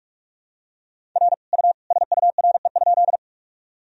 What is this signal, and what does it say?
Morse code, heard as a keyed tone, about 46 wpm, 700 Hz. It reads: R U SURE?
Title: Sentences from Top 300 Words and Abbreviations 45wpm